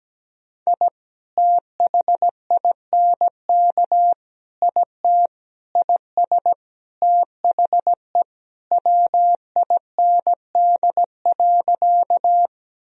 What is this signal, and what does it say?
Morse code, heard as a keyed tone, about 17 wpm, 700 Hz. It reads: I THINK IT IS THE WIND.
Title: Sentences from Top 1000 Words - Review Part 2 - 17wpm